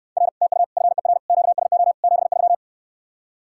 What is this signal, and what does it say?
Morse code, heard as a keyed tone, about 69 wpm, 700 Hz. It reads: R TU FER QSO 73